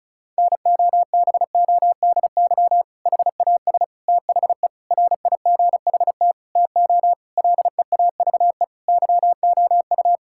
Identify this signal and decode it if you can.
Morse code, heard as a keyed tone, about 36 wpm, 700 Hz. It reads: NOBODY HAS THE RIGHT TO LEAVE YOU
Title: Sentences from Top 1600 Words 35wpm